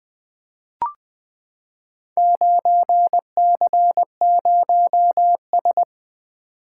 Morse code, heard as a keyed tone, about 20 wpm, 700 Hz. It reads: 9C0S
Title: Four Letter-Number Combinations - 20wpm